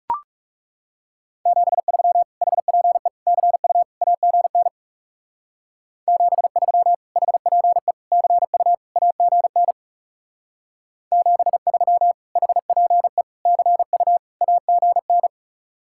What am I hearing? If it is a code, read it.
Morse code, heard as a keyed tone, about 35 wpm, 700 Hz. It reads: 73 HPE CU AGN 73 HPE CU AGN 73 HPE CU AGN
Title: Speed-Racing Common QSO Phrases Encore - 30wpm